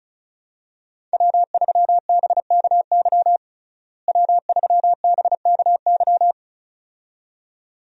Morse code, heard as a keyed tone, about 36 wpm, 700 Hz. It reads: W3BKY W3BKY
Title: USA Call Signs - 5 Characters - Repeated 2x - 35wpm